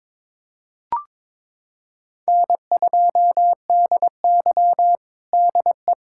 Morse code, heard as a keyed tone, about 22 wpm, 700 Hz. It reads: N2DY DE
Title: USA Callsign DE Callsign - 4 Characters - 22wpm